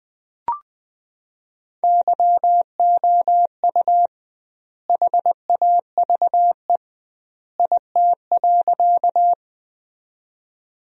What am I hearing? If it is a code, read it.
Morse code, heard as a keyed tone, about 20 wpm, 700 Hz. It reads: YOU HAVE IT.
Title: 2x Word-Spacing - Sentences from Top 100 Words - 20wpm